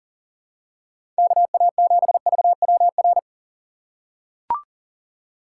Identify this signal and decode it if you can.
Morse code, heard as a keyed tone, about 40 wpm, 700 Hz. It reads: KA7VWR
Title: USA Call Signs - 6 Characters - 6x Character Spacing - 40wpm